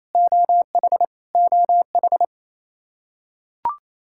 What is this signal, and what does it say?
Morse code, heard as a keyed tone, about 27 wpm, 700 Hz. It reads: OH OH
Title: Practice QSO - Exchange 1 - Beginning - 28wpm